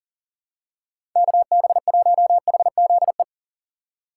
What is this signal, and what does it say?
Morse code, heard as a keyed tone, about 40 wpm, 700 Hz. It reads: KB1HZE
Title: USA Call Signs - 6 Characters - 6x Character Spacing - 40wpm